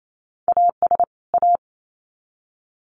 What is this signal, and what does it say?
Morse code, heard as a keyed tone, about 28 wpm, 700 Hz. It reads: AS A
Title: Sets of 2 Words 28wpm